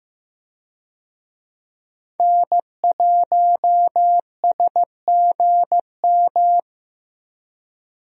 Morse code, heard as a keyed tone, about 15 wpm, 700 Hz. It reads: N1SGM
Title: USA Call Signs - 5 Characters - Repeated 2x - 15wpm